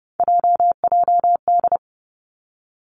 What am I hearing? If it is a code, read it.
Morse code, heard as a keyed tone, about 31 wpm, 700 Hz. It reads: JJD
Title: Three Challenging Letter-Number Combinations 30wpm